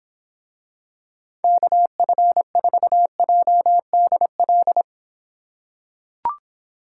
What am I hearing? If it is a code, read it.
Morse code, heard as a keyed tone, about 26 wpm, 700 Hz. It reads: KF4JDL